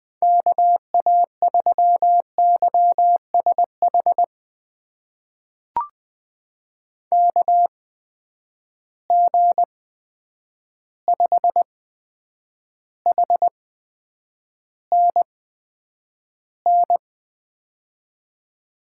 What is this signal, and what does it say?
Morse code, heard as a keyed tone, about 20 wpm, spaced as for 9 wpm, 700 Hz. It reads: KA3YSH KG5HNN
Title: USA Call Signs - 6 Characters - 8x Character Spacing - 20wpm